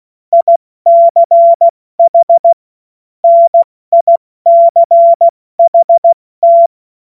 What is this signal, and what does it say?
Morse code, heard as a keyed tone, about 16 wpm, spaced as for 14 wpm, 680 Hz. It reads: ICH NICHT
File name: Morsekurs / Witze / Witz 10 Juni Mai 2025 70 F80